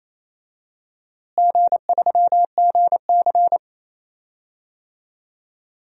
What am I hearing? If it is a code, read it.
Morse code, heard as a keyed tone, about 28 wpm, 700 Hz. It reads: G3GC
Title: DXCC-100 Call Signs - 4 Characters - Repeated 3x - 28wpm